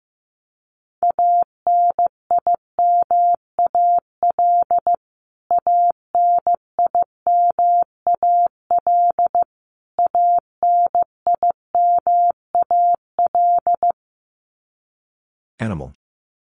Morse code, heard as a keyed tone, about 15 wpm, 700 Hz. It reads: ANIMAL ANIMAL ANIMAL
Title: Top 200 Words Repeated 15wpm